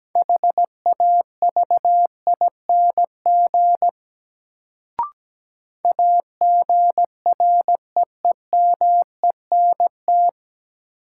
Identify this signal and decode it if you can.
Morse code, heard as a keyed tone, about 17 wpm, 700 Hz. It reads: HAVING AGREEMENT
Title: Sets of 5 Words Encore - New Word Drill - 17wpm